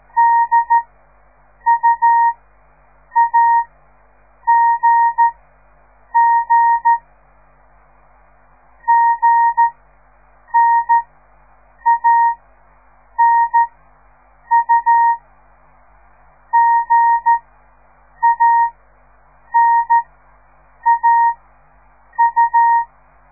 Morse code, heard as a keyed tone, about 13 wpm, spaced as for 7 wpm, 930 Hz. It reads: DUAGG GNANU GANAU
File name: radiomorningstar / morse / les 4